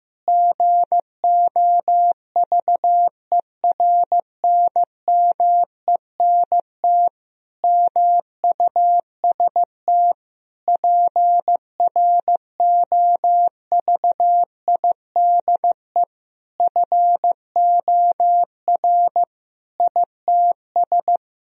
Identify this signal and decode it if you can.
Morse code, heard as a keyed tone, about 15 wpm, 700 Hz. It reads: GOVERNMENT MUST PROVIDE FOR ITS
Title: Sentences from Top 1000 Words - Review Part 3 - 15wpm